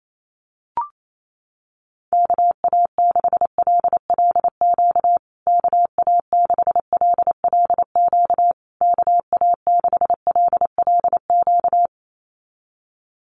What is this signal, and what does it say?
Morse code, heard as a keyed tone, about 28 wpm, 690 Hz. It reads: KA6LLQ KA6LLQ KA6LLQ
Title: USA Call Signs Repeated 3x 28wpm